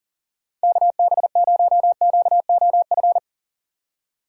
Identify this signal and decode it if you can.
Morse code, heard as a keyed tone, about 40 wpm, 700 Hz. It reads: KB0QOF
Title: USA Call Signs - 6x Character Spacing - 40wpm